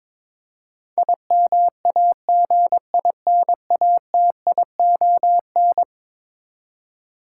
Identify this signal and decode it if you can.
Morse code, heard as a keyed tone, about 22 wpm, 700 Hz. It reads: IMAGINATION
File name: practice / 22wpm / Eleven Letter Words - 22wpm